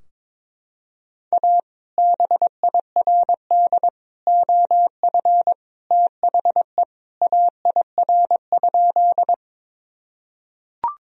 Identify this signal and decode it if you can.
Morse code, heard as a keyed tone, about 22 wpm, 700 Hz. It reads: A BIRD OF THE AIR?
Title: Sentences from Top 1000 Words - Review Part 2 - 22wpm